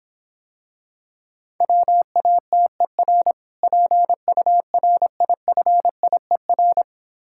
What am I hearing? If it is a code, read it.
Morse code, heard as a keyed tone, about 26 wpm, 700 Hz. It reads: WATER PURIFIER